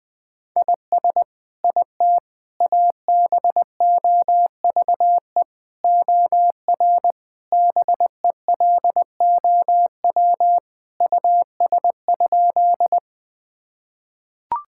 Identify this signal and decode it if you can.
Morse code, heard as a keyed tone, about 20 wpm, 700 Hz. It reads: IS IT ABOVE OR BELOW US?
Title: Sentences from Top 1000 Words - Review Part 5 - 20wpm